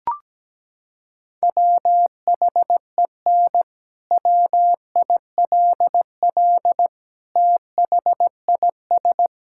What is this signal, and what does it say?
Morse code, heard as a keyed tone, about 17 wpm, 700 Hz. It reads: WHEN WILL THIS
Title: Sentences from Top 1600 Words 17wpm